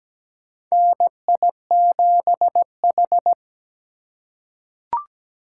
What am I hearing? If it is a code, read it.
Morse code, heard as a keyed tone, about 17 wpm, 700 Hz. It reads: NI7H
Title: USA Call Signs - 4 Characters - 4x Character Spacing - 17wpm